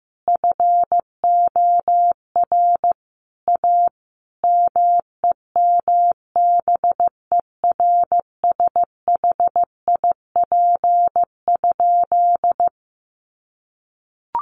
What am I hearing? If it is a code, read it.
Morse code, heard as a keyed tone, about 14 wpm, 700 Hz. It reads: FOR A MEMBERSHIP?
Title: Sentences from Top 1400 Words 15wpm